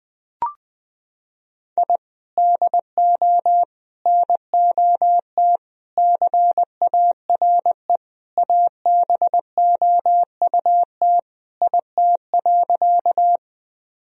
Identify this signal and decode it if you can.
Morse code, heard as a keyed tone, about 21 wpm, 700 Hz. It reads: I DO NOT CARE ABOUT IT.
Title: Sentences from Top 1000 Words - Review Part 1 - 20wpm